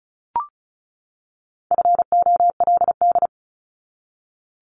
Morse code, heard as a keyed tone, about 34 wpm, 700 Hz. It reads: FOLD
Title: Top 1751-2000 Words 35wpm